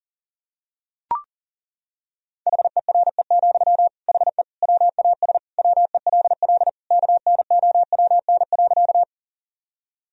Morse code, heard as a keyed tone, about 37 wpm, 700 Hz. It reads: HERE, HE WAS WELL KNOWN.
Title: Sentences from Top 1000 Words - Review Part 3 - 40wpm